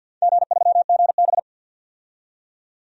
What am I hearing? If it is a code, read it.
Morse code, heard as a keyed tone, about 53 wpm, 700 Hz. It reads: G3GB
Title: DXCC-100 Call Signs - 4 Characters - 50wpm